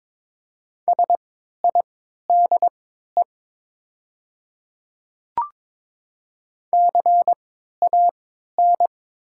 Morse code, heard as a keyed tone, about 22 wpm, spaced as for 13 wpm, 700 Hz. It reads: SIDE CAN
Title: 3x Character Spacing - Top 100 Words - 22wpm